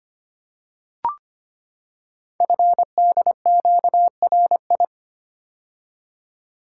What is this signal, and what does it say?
Morse code, heard as a keyed tone, about 25 wpm, 700 Hz. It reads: FDQRI
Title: Code Groups - 5 Letters - 25wpm